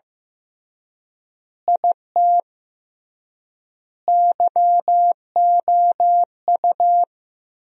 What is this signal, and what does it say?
Morse code, heard as a keyed tone, about 15 wpm, 700 Hz. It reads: IT YOU
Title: 3x Word-Spacing - Sentences from Top 200 Words - 15wpm